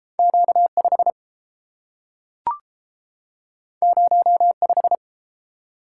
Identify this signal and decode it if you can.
Morse code, heard as a keyed tone, about 30 wpm, 700 Hz. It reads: Q 5 0 5